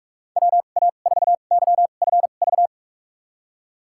Morse code, heard as a keyed tone, about 46 wpm, spaced as for 32 wpm, 700 Hz. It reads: WA4YFV
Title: USA Call Signs - 6 Characters - 2x Character Spacing - 45wpm